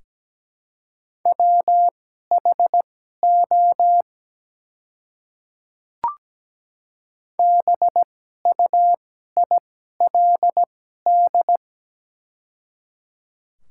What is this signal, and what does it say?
Morse code, heard as a keyed tone, about 17 wpm, spaced as for 12 wpm, 700 Hz. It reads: WHO BUILD
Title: 2x Character Spacing - Top 200 Words - 17wpm